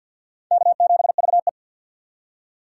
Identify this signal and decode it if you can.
Morse code, heard as a keyed tone, about 46 wpm, 700 Hz. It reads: K7VE
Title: USA Call Signs - 4 Characters - 3x Character Spacing - 50wpm